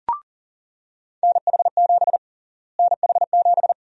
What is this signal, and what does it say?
Morse code, heard as a keyed tone, about 40 wpm, 700 Hz. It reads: NH7 NH7
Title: USA Call Sign Prefixes - Repeated 2x - 40wpm